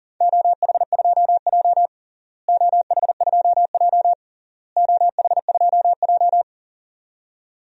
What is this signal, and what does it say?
Morse code, heard as a keyed tone, about 40 wpm, 700 Hz. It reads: OH2J OH2J OH2J
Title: DXCC-100 Call Signs - 4 Characters - Repeated 3x - 40wpm